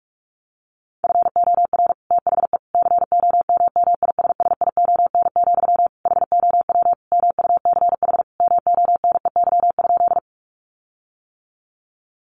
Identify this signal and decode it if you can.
Morse code, heard as a keyed tone, about 44 wpm, 700 Hz. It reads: FOR THE COMMISSION, HOW MUCH MONEY?
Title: Sentences from Top 1400 Words 45wpm